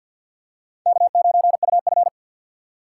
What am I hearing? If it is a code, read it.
Morse code, heard as a keyed tone, about 52 wpm, 700 Hz. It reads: K9UF
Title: USA Call Signs - 3x Character Spacing - 50wpm